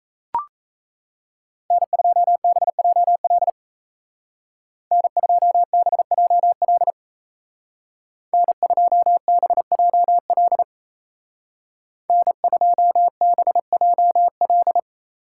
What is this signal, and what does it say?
Morse code, heard as a keyed tone, about 34 wpm, 700 Hz. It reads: N2BJL N2BJL N2BJL N2BJL
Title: Speed Racing - USA Call Signs 28wpm